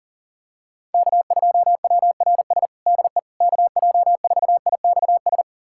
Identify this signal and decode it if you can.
Morse code, heard as a keyed tone, about 40 wpm, 700 Hz. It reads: K2WRS DE KJ4IXS
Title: USA Callsign DE Callsign 40wpm